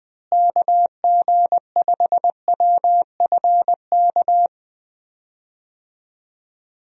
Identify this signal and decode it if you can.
Morse code, heard as a keyed tone, about 20 wpm, 700 Hz. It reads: KG5WFK
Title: USA Call Signs - 6 Characters - Repeated 2x - 20wpm